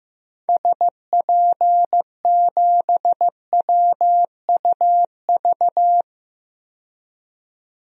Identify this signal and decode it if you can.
Morse code, heard as a keyed tone, about 15 wpm, 700 Hz. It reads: SP7WUV
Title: DXCC-100 Call Signs - 6 Characters - 15wpm